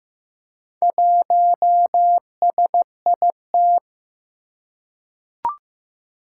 Morse code, heard as a keyed tone, about 15 wpm, 700 Hz. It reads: 1SIT
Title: Four Challenging Letter-Number Combinations 15wpm